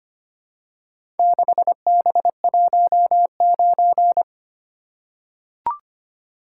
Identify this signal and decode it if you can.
Morse code, heard as a keyed tone, about 25 wpm, 700 Hz. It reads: 6B19
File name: practice / 25wpm / Four Letter-Number Combinations - 25wpm